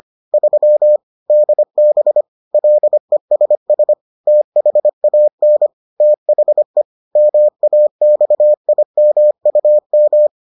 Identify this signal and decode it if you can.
Morse code, heard as a keyed tone, about 25 wpm, 600 Hz. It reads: 3 DB LESS THAN THE MAXIMUM